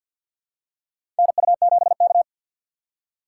Morse code, heard as a keyed tone, about 52 wpm, 700 Hz. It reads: NU7K
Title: USA Call Signs - 4 Characters - 2x Character Spacing - 50wpm